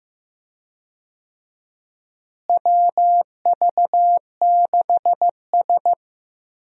Morse code, heard as a keyed tone, about 15 wpm, 700 Hz. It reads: WV6S
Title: USA Call Signs - 4 Characters - Repeated 3x - 15wpm